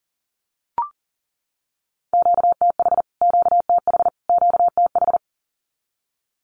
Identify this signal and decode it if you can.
Morse code, heard as a keyed tone, about 40 wpm, 700 Hz. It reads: QTH QTH QTH
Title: Q-Codes Repeated 3x 40wpm